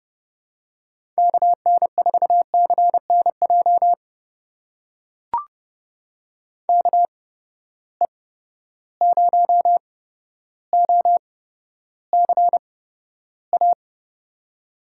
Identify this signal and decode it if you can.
Morse code, heard as a keyed tone, about 30 wpm, spaced as for 13 wpm, 700 Hz. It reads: KN4CNJ KE0OCA
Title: USA Call Signs - 6 Characters - 8x Character Spacing - 30wpm